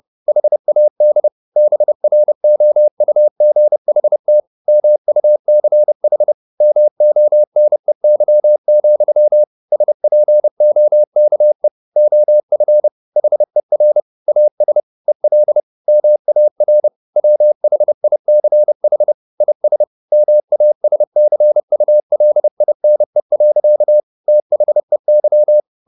Morse code, heard as a keyed tone, about 30 wpm, 600 Hz. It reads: HAD BROUGHT MUCH MONEY, SPOKE OF HER AS EL MAR WHICH IS MASCULINE. THEY
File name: the old man and the sea / the old man and the sea 1 30wpm